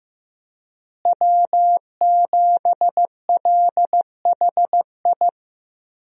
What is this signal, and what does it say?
Morse code, heard as a keyed tone, about 15 wpm, 700 Hz. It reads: W7LHI
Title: USA Call Signs - 5 Characters - Repeated 3x - 15wpm